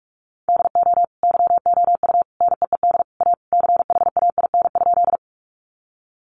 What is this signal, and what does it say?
Morse code, heard as a keyed tone, about 44 wpm, 700 Hz. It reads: DO YOU NEED A CHAIN?
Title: Sentences from Top 1500 Words 45wpm